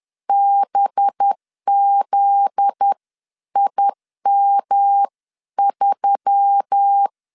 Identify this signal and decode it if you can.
Morse code, heard as a keyed tone, about 10 wpm, spaced as for 9 wpm, 800 Hz. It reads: BZIM3
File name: LEZIONI / 7 GF9Z / 7 GF9Z 10wpm 2611